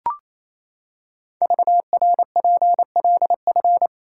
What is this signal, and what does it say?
Morse code, heard as a keyed tone, about 27 wpm, 700 Hz. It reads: VRPLF
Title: Code Groups - 5 Letters - 28wpm